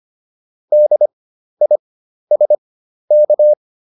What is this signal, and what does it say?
Morse code, heard as a keyed tone, about 25 wpm, spaced as for 12 wpm, 600 Hz. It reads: DISK